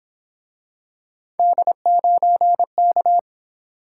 Morse code, heard as a keyed tone, about 28 wpm, 700 Hz. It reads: D9K
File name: Speed-Racing - Three Letter-Number Combinations - 22wpm